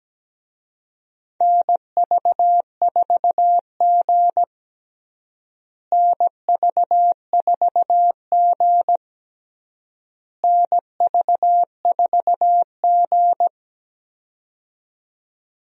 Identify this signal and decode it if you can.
Morse code, heard as a keyed tone, about 17 wpm, 700 Hz. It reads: NV4G NV4G NV4G
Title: USA Call Signs - 4 Characters - Repeated 3x - 17wpm